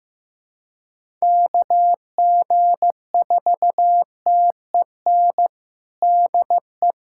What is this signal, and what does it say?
Morse code, heard as a keyed tone, about 15 wpm, 700 Hz. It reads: KG4TEN DE
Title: USA Callsign DE Callsign - 6 Characters - 15wpm